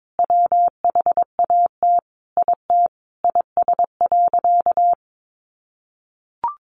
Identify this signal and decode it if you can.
Morse code, heard as a keyed tone, about 22 wpm, 700 Hz. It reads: WHAT IT IS.
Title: Sentences from Top 200 Words 22wpm